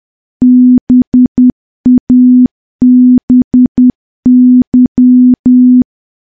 Morse code, heard as a keyed tone, about 10 wpm, 250 Hz. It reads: BABY
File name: morse / morse-training-10WPM